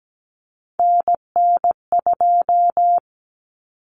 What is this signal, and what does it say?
Morse code, heard as a keyed tone, about 17 wpm, 700 Hz. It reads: NN2
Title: Common Cut Numbers - 3-Digit 17wpm